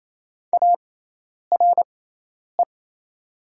Morse code, heard as a keyed tone, about 28 wpm, spaced as for 10 wpm, 700 Hz. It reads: ARE
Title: 6x Character Spacing - Top 300 Words - 28wpm